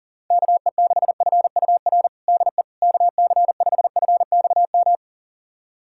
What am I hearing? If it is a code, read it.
Morse code, heard as a keyed tone, about 37 wpm, 700 Hz. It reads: KE6FUR DE KC5FXM
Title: USA Callsign DE Callsign 40wpm